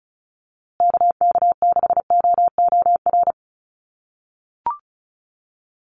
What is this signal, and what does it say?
Morse code, heard as a keyed tone, about 34 wpm, 700 Hz. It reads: KK6OOR
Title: USA Call Signs Repeated 3x 35wpm